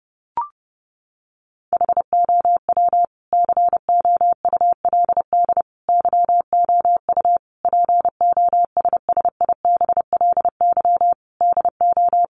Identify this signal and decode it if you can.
Morse code, heard as a keyed tone, about 30 wpm, 700 Hz. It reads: HOW COULD YOU POSSIBLY DO
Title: Sentences from Top 1300 Words 30wpm